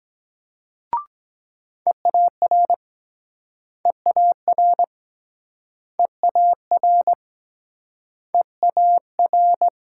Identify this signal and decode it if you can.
Morse code, heard as a keyed tone, about 23 wpm, 700 Hz. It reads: EAR EAR EAR EAR